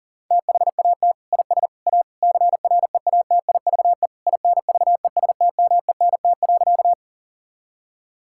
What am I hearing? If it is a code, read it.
Morse code, heard as a keyed tone, about 40 wpm, 700 Hz. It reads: THAT IS A CREATIVE INVESTMENT.
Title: Sentences from Top 1000 Words - Review Part 5 - 40wpm